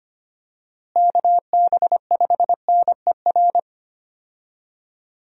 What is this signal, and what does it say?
Morse code, heard as a keyed tone, about 25 wpm, 700 Hz. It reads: KB5NER
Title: USA Call Signs - 6 Characters - Repeated 2x - 25wpm